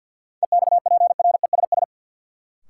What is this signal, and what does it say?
Morse code, heard as a keyed tone, about 49 wpm, 700 Hz. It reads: EXPRESS
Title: Sentences from Top 1100 Words - New Word Drill Alt - 50wpm